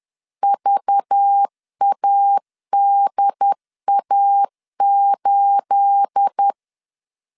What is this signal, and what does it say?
Morse code, heard as a keyed tone, about 10 wpm, 800 Hz. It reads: VADA8
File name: CW / DET / 1 9 1500C S B 10wpm 4439